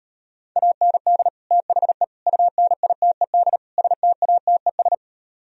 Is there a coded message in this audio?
Yes: Morse code from 0.6 to 4.9 s, heard as a keyed tone, about 39 wpm, 700 Hz. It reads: AND THE UNITED STATES